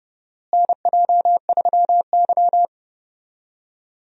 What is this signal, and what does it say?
Morse code, heard as a keyed tone, about 30 wpm, 700 Hz. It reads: NJ3Y
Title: USA Call Signs - 4 Characters - Repeated 3x - 30wpm